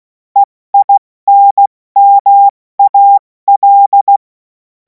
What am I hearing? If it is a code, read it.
Morse code, heard as a keyed tone, about 16 wpm, 800 Hz. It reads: EINMAL
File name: Morsekurs / Witze / Witz 24. Februar 2026 70 F80